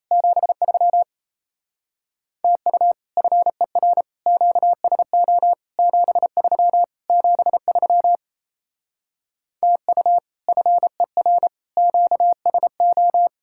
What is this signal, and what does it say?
Morse code, heard as a keyed tone, about 34 wpm, 700 Hz. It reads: 73 TU FER QSO 73 73 TU FER QSO
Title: Speed-Racing Common QSO Phrases - 28wpm